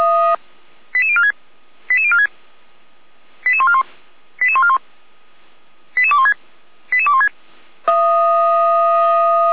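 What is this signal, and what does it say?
Five-tone selective call (ZVEI), heard as quick runs of five tones, tones of about 70 ms. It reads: 88046 88046 88141 88141 88216 88216
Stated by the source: ZVEI five-tone call received by SDR, archived by Feuerwehr Orlamünde.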